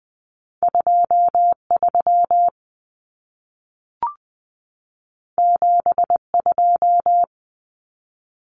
Morse code, heard as a keyed tone, about 20 wpm, 700 Hz. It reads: 23 72
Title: Common Cut Numbers - 2-Digit 20wpm